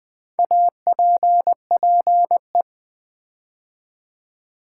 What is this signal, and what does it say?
Morse code, heard as a keyed tone, about 20 wpm, 700 Hz. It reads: APPE
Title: Build A Word - 4 Letters At A Time - 10-Letter Words - 20wpm